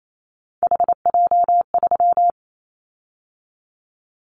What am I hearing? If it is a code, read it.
Morse code, heard as a keyed tone, about 28 wpm, 700 Hz. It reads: HJ3
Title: Callsign Prefixes for Top 100 DXCC - 28wpm